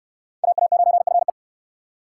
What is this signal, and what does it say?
Morse code, heard as a keyed tone, about 59 wpm, 700 Hz. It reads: AA9FE